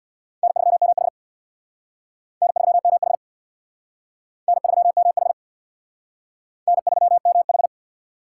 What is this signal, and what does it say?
Morse code, heard as a keyed tone, about 60 wpm, 690 Hz. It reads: N3MH N3MH N3MH N3MH
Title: Speed Racing - USA 1x2 Call Signs 50wpm